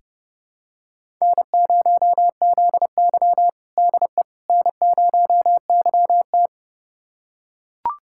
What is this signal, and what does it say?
Morse code, heard as a keyed tone, about 30 wpm, 700 Hz. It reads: N0ZY DE N0YT
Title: USA Callsign DE Callsign - 4 Characters - 30wpm